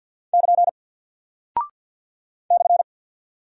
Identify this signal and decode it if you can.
Morse code, heard as a keyed tone, about 49 wpm, 700 Hz. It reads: <KN> /
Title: Single Character (Letter:Number:Punctuation:Prosign) Alt 50wpm